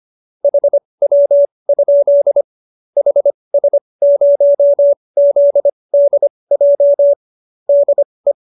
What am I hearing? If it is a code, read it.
Morse code, heard as a keyed tone, about 25 wpm, spaced as for 20 wpm, 570 Hz. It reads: HW? HS0ZDJ DE